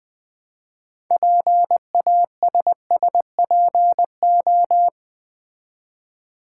Morse code, heard as a keyed tone, about 20 wpm, 700 Hz. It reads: PASSPO
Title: Build A Word - 3 Letters At A Time - 8-Letter Words - 20wpm